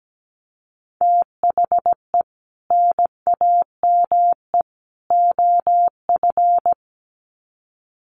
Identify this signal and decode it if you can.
Morse code, heard as a keyed tone, about 17 wpm, 700 Hz. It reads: THE NAME OF
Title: Sets of 3 Words 17wpm